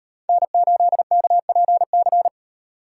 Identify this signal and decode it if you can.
Morse code, heard as a keyed tone, about 36 wpm, 700 Hz. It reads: N8KPC